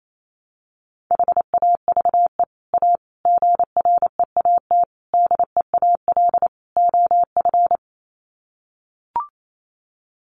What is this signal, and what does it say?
Morse code, heard as a keyed tone, about 28 wpm, 700 Hz. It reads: HAVE A GREAT DEAL OF
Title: Sets of 5 Words Encore2 28wpm